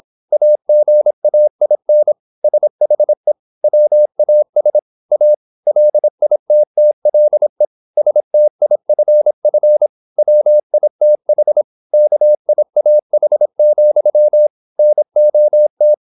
Morse code, heard as a keyed tone, about 26 wpm, 600 Hz. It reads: AGAIN SHE WAS A LITTLE STIFF WITH KIAH, NOT
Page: Never 26wpm | CW med Gnister